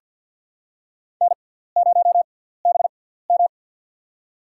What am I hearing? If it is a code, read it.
Morse code, heard as a keyed tone, about 46 wpm, spaced as for 17 wpm, 700 Hz. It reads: N0BM
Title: USA Call Signs - 4 Characters - 6x Character Spacing - 50wpm